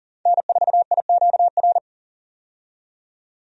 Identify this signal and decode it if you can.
Morse code, heard as a keyed tone, about 37 wpm, 700 Hz. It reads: N4IQR
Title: USA Call Signs - 5 Characters - 40wpm